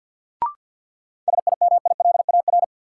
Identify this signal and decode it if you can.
Morse code, heard as a keyed tone, about 51 wpm, 700 Hz. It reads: SIMILAR
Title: Sentences from Top 1000 Words - Review Part 5 - New Word Drill - 50wpm